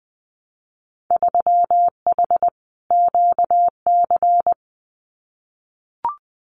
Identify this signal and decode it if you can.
Morse code, heard as a keyed tone, about 20 wpm, 700 Hz. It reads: 3H QC
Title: Winter Field Day Exchanges 20wpm